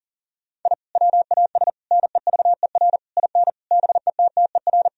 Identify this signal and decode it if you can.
Morse code, heard as a keyed tone, about 40 wpm, 700 Hz. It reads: I WAS NEVER IN BETTER
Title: Sentences from Top 1000 Words - Review Part 3 - 40wpm